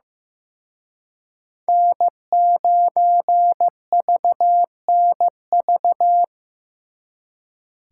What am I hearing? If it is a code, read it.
Morse code, heard as a keyed tone, about 18 wpm, 700 Hz. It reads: N9VNV
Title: Speed Racing - USA 1x3 Call Signs 15wpm